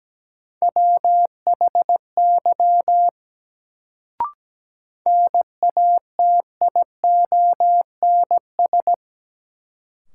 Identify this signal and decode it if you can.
Morse code, heard as a keyed tone, about 17 wpm, 700 Hz. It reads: WHY NATIONS
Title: Sets of 3 Words - New Word Drill - 17wpm